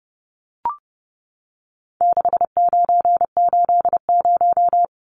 Morse code, heard as a keyed tone, about 30 wpm, 700 Hz. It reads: 6980
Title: Numbers - 4-Digit 30wpm